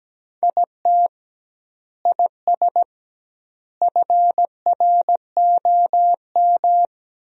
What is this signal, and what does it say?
Morse code, heard as a keyed tone, about 17 wpm, 700 Hz. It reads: IT IS FROM
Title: 2x Word-Spacing - Sentences from Top 300 Words - 17wpm